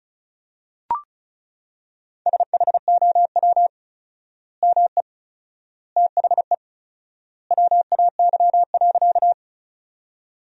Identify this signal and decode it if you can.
Morse code, heard as a keyed tone, about 36 wpm, 700 Hz. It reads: SHOW ME THE WAY.
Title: 4x Word-Spacing - Sentences from Top 200 Words - 35wpm